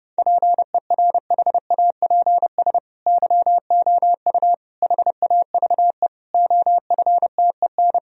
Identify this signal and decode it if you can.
Morse code, heard as a keyed tone, about 30 wpm, 700 Hz. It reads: PERHAPS YOU HAVE OFTEN
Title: Sentences from Top 1000 Words - Review Part 1 - 30wpm